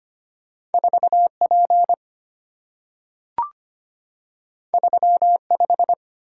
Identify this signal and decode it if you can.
Morse code, heard as a keyed tone, about 25 wpm, 690 Hz. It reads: 4P 35
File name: Two Letter-Number Combinations - 25wpm